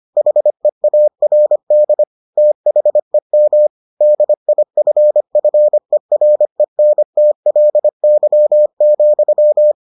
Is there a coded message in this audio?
Yes: Morse code from 0.2 to 9.7 s, heard as a keyed tone, about 25 wpm, 600 Hz. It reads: HEARD THEM DIFFERENTLY,